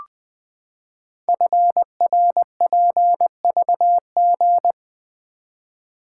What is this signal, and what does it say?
Morse code, heard as a keyed tone, about 21 wpm, 700 Hz. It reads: FRPVG
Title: Code Groups - 5 Letters - 20wpm